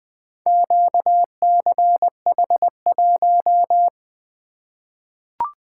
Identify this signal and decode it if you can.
Morse code, heard as a keyed tone, about 20 wpm, 700 Hz. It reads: QCH1
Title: Four Letter-Number Combinations - 20wpm